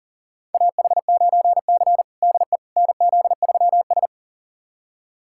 Morse code, heard as a keyed tone, about 37 wpm, 700 Hz. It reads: AH9C DE NZ3S
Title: USA Callsign DE Callsign - 4 Characters - 40wpm